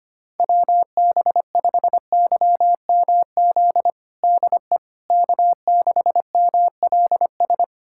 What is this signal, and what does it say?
Morse code, heard as a keyed tone, about 25 wpm, 700 Hz. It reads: WB5YMZ DE K6MLS
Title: USA Callsign DE Callsign 25wpm